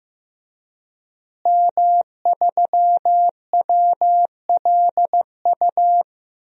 Morse code, heard as a keyed tone, about 16 wpm, 700 Hz. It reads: M3WLU
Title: DXCC-100 Call Signs - 5 Characters - Repeated 3x - 15wpm